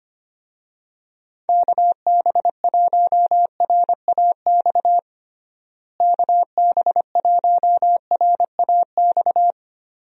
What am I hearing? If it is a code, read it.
Morse code, heard as a keyed tone, about 25 wpm, 700 Hz. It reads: KB1RAX KB1RAX
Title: USA Call Signs - 6 Characters - Repeated 2x - 25wpm